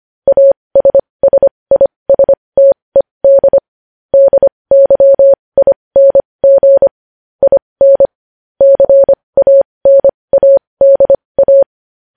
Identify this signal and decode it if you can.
Morse code, heard as a keyed tone, about 25 wpm, spaced as for 20 wpm, 550 Hz. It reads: ASSISTED DYING IN CANADA
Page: News headlines in morse code at 20 wpm